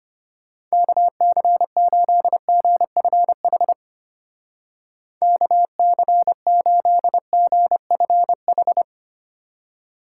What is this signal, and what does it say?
Morse code, heard as a keyed tone, about 32 wpm, 700 Hz. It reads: KC8GFH KC8GFH